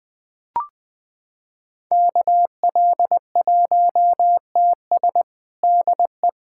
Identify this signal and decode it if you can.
Morse code, heard as a keyed tone, about 20 wpm, 700 Hz. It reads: KL1TS DE
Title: USA Callsign DE Callsign 20wpm